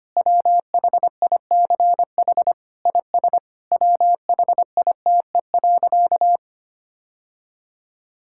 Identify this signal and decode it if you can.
Morse code, heard as a keyed tone, about 25 wpm, 700 Hz. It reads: WHICH IS WHITE.
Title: Sentences from Top 1600 Words 25wpm